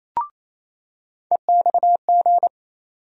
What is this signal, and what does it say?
Morse code, heard as a keyed tone, about 27 wpm, 700 Hz. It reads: EXG
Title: USA Call Sign Postfixes - 3 Letters 28wpm